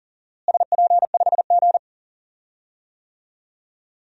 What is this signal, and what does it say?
Morse code, heard as a keyed tone, about 40 wpm, 700 Hz. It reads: SP5G
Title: DXCC-100 Call Signs - 4 Characters - Repeated 3x - 40wpm